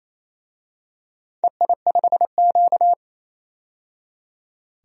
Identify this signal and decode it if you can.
Morse code, heard as a keyed tone, about 28 wpm, 700 Hz. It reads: EI5Q
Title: DXCC-100 Call Signs - 4 Characters - 28wpm